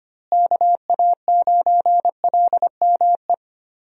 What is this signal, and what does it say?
Morse code, heard as a keyed tone, about 25 wpm, 700 Hz. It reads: KA9LME
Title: USA Call Signs - 6 Characters - Repeated 3x - 25wpm